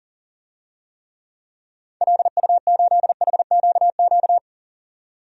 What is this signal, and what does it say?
Morse code, heard as a keyed tone, about 40 wpm, 700 Hz. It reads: LU8HQQ
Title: DXCC-100 Call Signs - 6 Characters - 40wpm